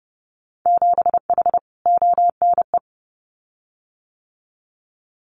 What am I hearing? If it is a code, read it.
Morse code, heard as a keyed tone, about 30 wpm, 700 Hz. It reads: 7H ONE
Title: Winter Field Day Exchanges Encore 30wpm